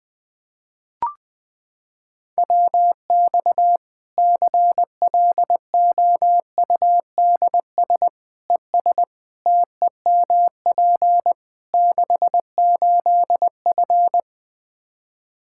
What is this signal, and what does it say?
Morse code, heard as a keyed tone, about 20 wpm, 700 Hz. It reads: WX CLOUDS ES TEMP 68F
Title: Practice QSO - Exchange 2 - Info - 20wpm